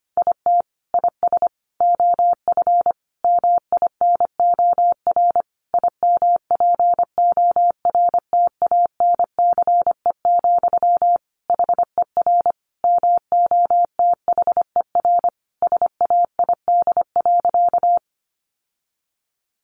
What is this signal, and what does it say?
Morse code, heard as a keyed tone, about 25 wpm, 700 Hz. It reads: IT IS OF MINOR IMPORTANCE, HER MOTHER SAID.
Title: Sentences from Top 1500 Words 25wpm